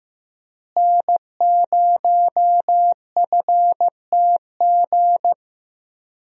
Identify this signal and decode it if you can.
Morse code, heard as a keyed tone, about 15 wpm, 700 Hz. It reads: N0FTG
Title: USA Call Signs - 5 Characters - Repeated 2x - 15wpm